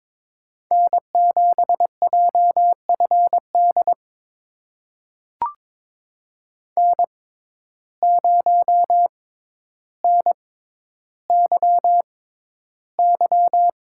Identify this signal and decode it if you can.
Morse code, heard as a keyed tone, about 22 wpm, spaced as for 11 wpm, 700 Hz. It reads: N7JFD N0NYY
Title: USA Call Signs - 5 Characters - 6x Character Spacing - 22wpm